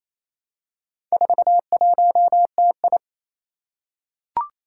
Morse code, heard as a keyed tone, about 27 wpm, 700 Hz. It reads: 41TI
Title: Four Challenging Letter-Number Combinations 28wpm